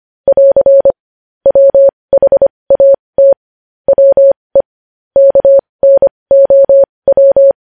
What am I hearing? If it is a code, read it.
Morse code, heard as a keyed tone, about 25 wpm, spaced as for 20 wpm, 550 Hz. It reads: <AR> WHAT WE KNOW
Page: news-headlines-in-morse-code-at-20-wpm